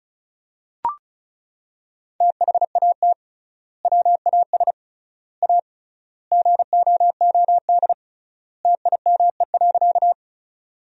Morse code, heard as a keyed tone, about 36 wpm, 700 Hz. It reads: THAT WAS A GOOD TIME.
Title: 3x Word-Spacing - Sentences from Top 200 Words - 35wpm